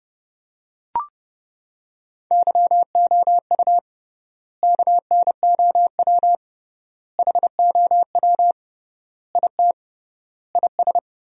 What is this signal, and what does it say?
Morse code, heard as a keyed tone, about 29 wpm, 700 Hz. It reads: YOU KNOW HOW IT IS
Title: 3x Word-Spacing - Sentences from Top 100 Words - 30wpm